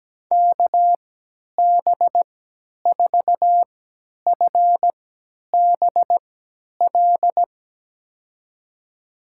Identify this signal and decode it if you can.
Morse code, heard as a keyed tone, about 17 wpm, spaced as for 10 wpm, 700 Hz. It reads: KB4FBL
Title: USA Call Signs - 6 Characters - 3x Character Spacing - 17wpm